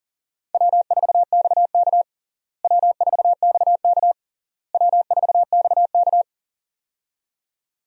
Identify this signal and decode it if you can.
Morse code, heard as a keyed tone, about 37 wpm, 700 Hz. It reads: W4XK W4XK W4XK
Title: USA Call Signs - 4 Characters - Repeated 3x - 40wpm